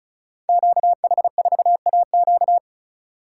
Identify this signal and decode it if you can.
Morse code, heard as a keyed tone, about 36 wpm, 700 Hz. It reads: QH4AQ
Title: Code Groups - 5 Letter:Numbers - 35wpm